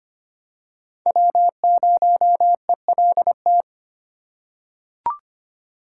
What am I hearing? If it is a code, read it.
Morse code, heard as a keyed tone, about 25 wpm, 700 Hz. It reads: W0ELT
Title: USA Call Signs - 3x Character Spacing - 25wpm